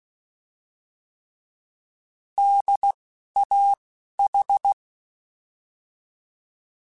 Morse code, heard as a keyed tone, about 16 wpm, 800 Hz. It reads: D A H
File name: Alphabetical / Alphabetical mp3s / Ds @10